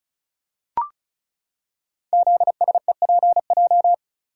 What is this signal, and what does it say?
Morse code, heard as a keyed tone, about 36 wpm, 700 Hz. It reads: ZSEPJ
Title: Code Groups - 5 Letters - 35wpm